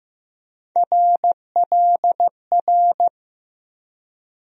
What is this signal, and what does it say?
Morse code, heard as a keyed tone, about 15 wpm, 700 Hz. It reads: RLR
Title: USA Call Sign Postfixes - 3 Letters 15wpm